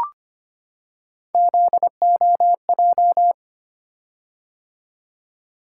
Morse code, heard as a keyed tone, about 25 wpm, 700 Hz. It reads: ZOJ
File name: Three Letter-Number Combinations - 25wpm